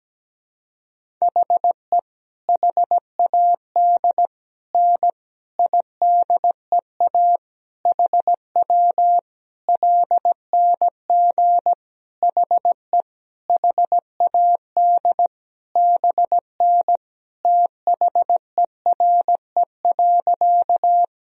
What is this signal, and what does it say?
Morse code, heard as a keyed tone, about 17 wpm, 700 Hz. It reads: HE HAD N IDEA HW LNG HE HAD BN THERE.
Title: Sentences from Top 300 Words and Abbreviations 17wpm